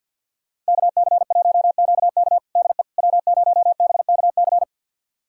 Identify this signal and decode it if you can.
Morse code, heard as a keyed tone, about 51 wpm, 700 Hz. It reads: KC1QK DE W0BKC
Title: USA Callsign DE Callsign - 5 Characters - 50wpm